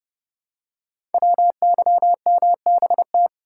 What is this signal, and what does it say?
Morse code, heard as a keyed tone, about 30 wpm, 690 Hz. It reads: WYMBT
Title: Code Groups - 5 Letters - 30wpm